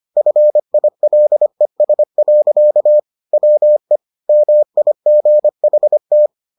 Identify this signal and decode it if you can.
Morse code, heard as a keyed tone, about 25 wpm, 600 Hz. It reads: FILES. WE MIGHT